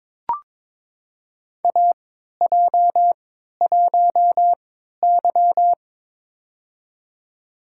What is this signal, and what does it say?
Morse code, heard as a keyed tone, about 22 wpm, spaced as for 13 wpm, 700 Hz. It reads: AJ1Y
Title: USA Call Signs - 4 Characters - 3x Character Spacing - 22wpm